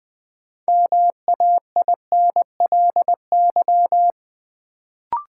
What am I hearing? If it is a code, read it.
Morse code, heard as a keyed tone, about 20 wpm, 700 Hz. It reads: MAINLY
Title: Sentences from Top 1400 Words - New Word Drill Alt - 20wpm